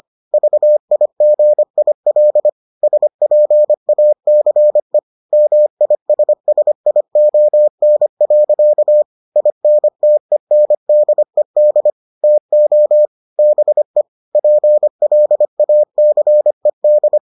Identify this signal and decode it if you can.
Morse code, heard as a keyed tone, about 25 wpm, 600 Hz. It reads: VIGIL SPACE MISSION. INTENDED TO BE PLACED